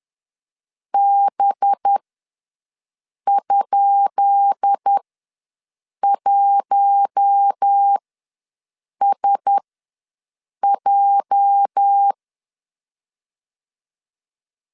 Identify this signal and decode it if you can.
Morse code, heard as a keyed tone, about 10 wpm, spaced as for 6 wpm, 800 Hz. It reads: B?1SJ
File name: LEZIONI / 10 SIGNS / 10D COMA ws 10wpm 4314